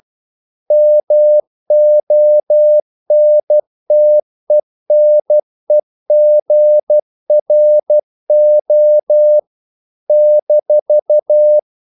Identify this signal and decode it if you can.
Morse code, heard as a keyed tone, about 12 wpm, 600 Hz. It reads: MONTENEGRO -
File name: CW / Practice / NationalCapitals12